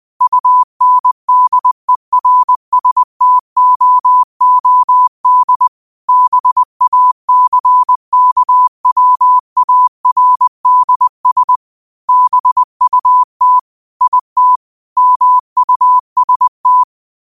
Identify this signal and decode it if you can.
Morse code, heard as a keyed tone, about 20 wpm, 1000 Hz. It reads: UNDERSTOOD BACKWARDS BUT IT MUST
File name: qotdfiles / 20wpm